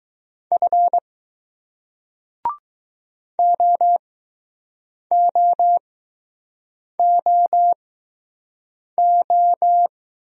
Morse code, heard as a keyed tone, about 20 wpm, spaced as for 6 wpm, 700 Hz. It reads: F OOOO